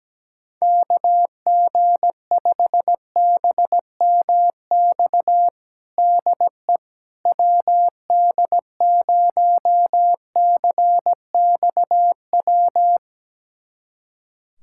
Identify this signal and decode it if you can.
Morse code, heard as a keyed tone, about 17 wpm, 700 Hz. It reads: KG5BMX DE WD0CXW
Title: USA Callsign DE Callsign - 6 Characters - 17wpm